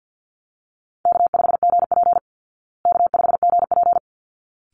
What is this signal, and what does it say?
Morse code, heard as a keyed tone, about 50 wpm, 700 Hz. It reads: K5GP K5GP
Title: Easy Call Signs Repeated 2x - 50wpm